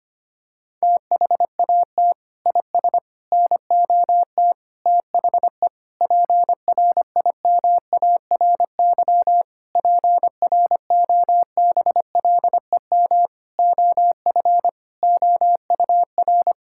Morse code, heard as a keyed tone, about 25 wpm, 700 Hz. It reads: THAT IS NOT THE PRIMARY PROBLEM OF OUR
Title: Sentences from Top 1000 Words - Review Part 5 - 25wpm